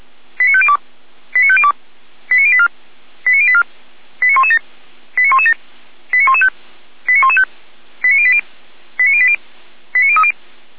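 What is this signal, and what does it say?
Five-tone selective call (ZVEI), heard as quick runs of five tones, tones of about 70 ms. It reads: 89502 89502 89084 89084 89117 89117 89115 89115 89080 89080 89030
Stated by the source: ZVEI five-tone selective call received on an SDR, logged by Feuerwehr Orlamünde fire brigade.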